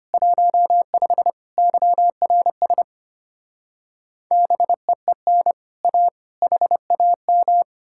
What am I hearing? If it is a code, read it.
Morse code, heard as a keyed tone, about 33 wpm, 700 Hz. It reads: 15 YRS BEEN A HAM